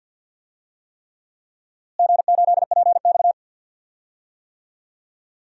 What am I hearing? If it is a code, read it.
Morse code, heard as a keyed tone, about 46 wpm, 700 Hz. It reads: G8PX
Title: DXCC-100 Call Signs - 4 Characters - Repeated 3x - 50wpm